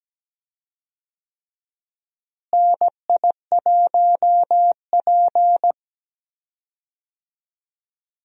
Morse code, heard as a keyed tone, about 17 wpm, 700 Hz. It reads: NI1P
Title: USA Call Signs - 4 Characters - Repeated 3x - 17wpm